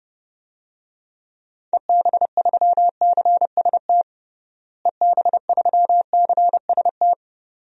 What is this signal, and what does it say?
Morse code, heard as a keyed tone, about 31 wpm, 700 Hz. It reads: EB3CST EB3CST
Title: DXCC-100 Call Signs - 6 Characters - Repeated 2x - 30wpm